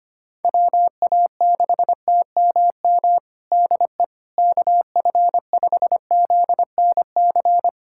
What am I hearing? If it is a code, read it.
Morse code, heard as a keyed tone, about 25 wpm, 700 Hz. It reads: WA6TMM DE KF5ZNC
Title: USA Callsign DE Callsign - 6 Characters - 25wpm